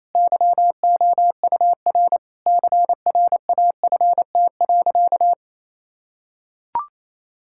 Morse code, heard as a keyed tone, about 28 wpm, 700 Hz. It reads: YOUR CRAFT.
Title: Sentences from Top 1000 Words - Review Part 6 - 28wpm